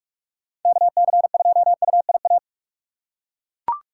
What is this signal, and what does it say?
Morse code, heard as a keyed tone, about 42 wpm, 700 Hz. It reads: KC2UIA